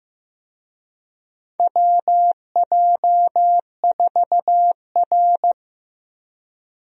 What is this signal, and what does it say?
Morse code, heard as a keyed tone, about 15 wpm, 700 Hz. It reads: WJ4R
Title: USA Call Signs - 4 Characters - Repeated 3x - 15wpm